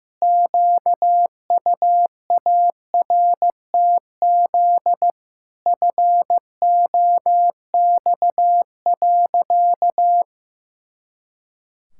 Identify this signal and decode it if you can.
Morse code, heard as a keyed tone, about 16 wpm, 700 Hz. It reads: QUARTZ FOX.
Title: Pangram Challenge v2 - 15wpm